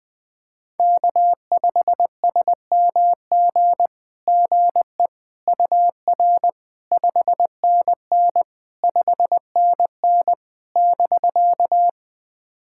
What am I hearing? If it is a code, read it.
Morse code, heard as a keyed tone, about 20 wpm, 700 Hz. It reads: K5SMG GE UR 5NN 5NN <BK>
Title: SOTA QSO Practice 20wpm